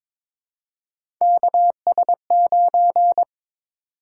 Morse code, heard as a keyed tone, about 22 wpm, 700 Hz. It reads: KS9
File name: practice / 22wpm / Three Letter-Number Combinations - 22wpm